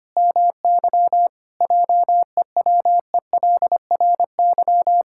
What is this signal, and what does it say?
Morse code, heard as a keyed tone, about 25 wpm, 700 Hz. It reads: MY JEWELRY
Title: Pangram Challenge v2 - 25wpm